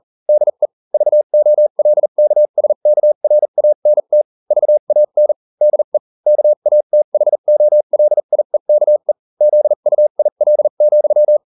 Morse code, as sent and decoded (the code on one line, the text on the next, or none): -.. / ...- --- .-.. -.- ... -.- .-. .- -. - / ...- .- -. / -.. / -.- .- - .... --- .-.. .. -.- . / --.. ..- .. .-.. --..--
DE VOLKSKRANT VAN DE KATHOLIEKE ZUIL,